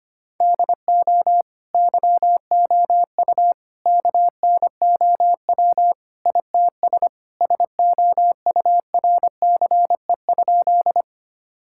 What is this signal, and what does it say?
Morse code, heard as a keyed tone, about 25 wpm, 700 Hz. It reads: DO YOU KNOW ITS SOURCE?
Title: Sentences from Top 1000 Words - Review Part 5 - 25wpm